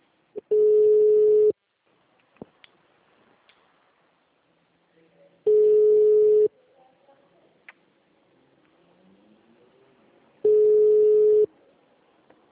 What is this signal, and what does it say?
Ringback tone (Europe, 425 Hz), with speech.